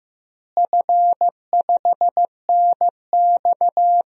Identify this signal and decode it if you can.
Morse code, heard as a keyed tone, about 15 wpm, 700 Hz. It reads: F5NX
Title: DXCC-100 Call Signs - 4 Characters - Repeated 2x - 15wpm